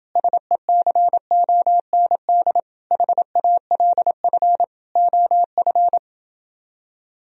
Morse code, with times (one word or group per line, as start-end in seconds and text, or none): 0.2-2.6 SECOND
2.9-4.6 HALF
5.0-6.0 OF